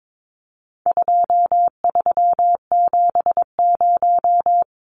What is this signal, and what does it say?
Morse code, heard as a keyed tone, about 22 wpm, 700 Hz. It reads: 2370
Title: Numbers - 4-Digit 22wpm